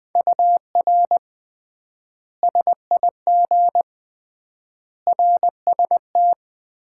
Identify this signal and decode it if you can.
Morse code, heard as a keyed tone, about 21 wpm, 700 Hz. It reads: UR SIG RST
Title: Common QSO Phrases - 3x Word-Spacing - 20wpm